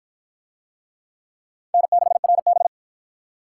Morse code, heard as a keyed tone, about 53 wpm, 700 Hz. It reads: N6RB